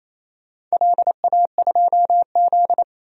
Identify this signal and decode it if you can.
Morse code, heard as a keyed tone, about 28 wpm, 700 Hz. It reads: LA2Z
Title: DXCC-100 Call Signs - 4 Characters - 28wpm